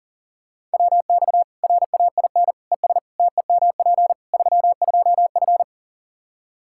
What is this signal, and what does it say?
Morse code, heard as a keyed tone, about 40 wpm, 700 Hz. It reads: WX RAIN ES TEMP 32F
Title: K1USN SST QSO Practice 40wpm